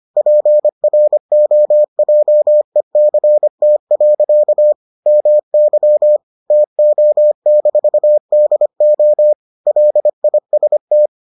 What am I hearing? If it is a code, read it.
Morse code, heard as a keyed tone, about 25 wpm, 600 Hz. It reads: PROJECT. MY TO-DO LIST